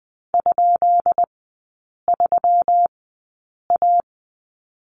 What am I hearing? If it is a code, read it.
Morse code, heard as a keyed tone, about 20 wpm, spaced as for 8 wpm, 690 Hz. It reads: ?3A
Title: 145 Lesson 24e - Letter y - Three Characters 20wpm